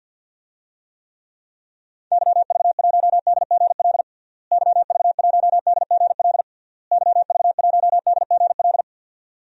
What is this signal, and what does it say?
Morse code, heard as a keyed tone, about 46 wpm, 700 Hz. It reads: YV1DGL YV1DGL YV1DGL
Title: DXCC-100 Call Signs - 6 Characters - Repeated 3x - 50wpm